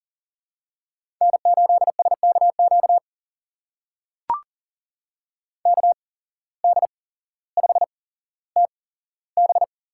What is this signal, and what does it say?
Morse code, heard as a keyed tone, about 41 wpm, spaced as for 17 wpm, 700 Hz. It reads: N8SKQ KD5TB
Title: USA Call Signs - 5 Characters - 8x Character Spacing - 40wpm